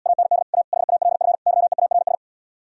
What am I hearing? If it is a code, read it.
Morse code, heard as a keyed tone, about 77 wpm, 700 Hz. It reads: AND A HALF YEARS